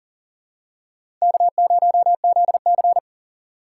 Morse code, heard as a keyed tone, about 40 wpm, 700 Hz. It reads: K0ZC
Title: USA Call Signs - 4 Characters - 2x Character Spacing - 40wpm